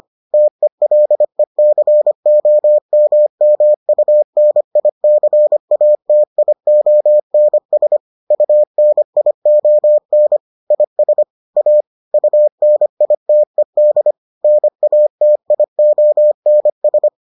Morse code, heard as a keyed tone, about 25 wpm, 600 Hz. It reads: TELECOMMUNICATIONS UNION IS A UNITED NATIONS